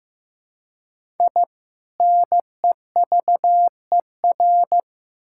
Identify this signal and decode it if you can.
Morse code, heard as a keyed tone, about 15 wpm, 700 Hz. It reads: I NEVER
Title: Sentences from Top 1000 Words - Review Part 6 - 15wpm